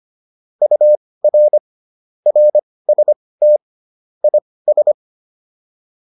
Morse code, heard as a keyed tone, about 25 wpm, spaced as for 18 wpm, 600 Hz. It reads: UR RST IS